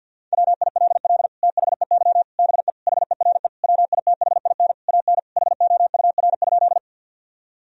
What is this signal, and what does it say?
Morse code, heard as a keyed tone, about 50 wpm, 700 Hz. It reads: WILL THEY BE HERE WITHIN AN HOUR?
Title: Sentences from Top 1000 Words - Review Part 4 - 50wpm